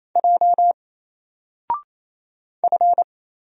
Morse code, heard as a keyed tone, about 27 wpm, 700 Hz. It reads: J F
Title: Single Character (Letter:Number:Punctuation:Prosign) 28wpm